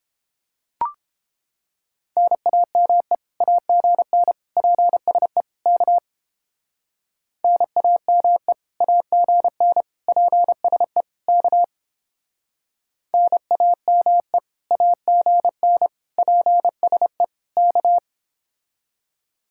Morse code, heard as a keyed tone, about 29 wpm, 700 Hz. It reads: NAME AGN PSE K NAME AGN PSE K NAME AGN PSE K